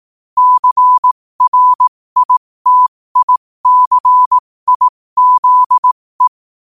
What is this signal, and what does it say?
Morse code, heard as a keyed tone, about 18 wpm, spaced as for 15 wpm, 1000 Hz. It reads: CRITICIZE
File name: qotdfiles / 15wpm